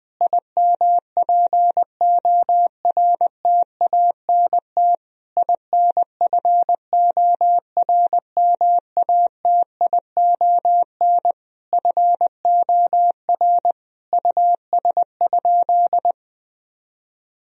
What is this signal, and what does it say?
Morse code, heard as a keyed tone, about 20 wpm, 700 Hz. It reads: IMPORTANT INFORMATION FOR US?
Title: Sentences from Top 1000 Words - Review Part 3 - 20wpm